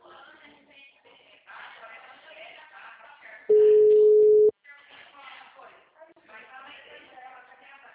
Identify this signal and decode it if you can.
Ringback tone (Europe, 425 Hz), with speech.